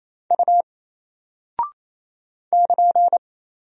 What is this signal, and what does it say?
Morse code, heard as a keyed tone, about 27 wpm, 700 Hz. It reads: U <KN>
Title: Single Character (Letter:Number:Punctuation:Prosign) 28wpm